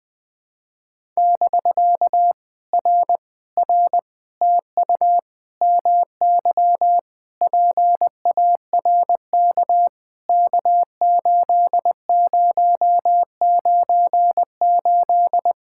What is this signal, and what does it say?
Morse code, heard as a keyed tone, about 20 wpm, 700 Hz. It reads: <BK> R R TU MY PARK K8098
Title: POTA QSO Practice - 20wpm